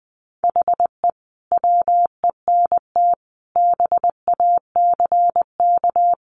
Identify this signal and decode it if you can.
Morse code, heard as a keyed tone, about 20 wpm, 700 Hz. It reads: HE WENT BACK
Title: Sentences from Top 1100 Words 20wpm